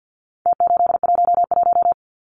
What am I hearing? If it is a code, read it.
Morse code, heard as a keyed tone, about 52 wpm, 700 Hz. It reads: T811
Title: Numbers - Contest Style - 4-Digit 50wpm